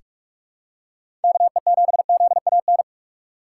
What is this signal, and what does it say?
Morse code, heard as a keyed tone, about 46 wpm, 700 Hz. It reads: KE7ZAN
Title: USA Call Signs - 6 Characters - 8x Character Spacing - 45wpm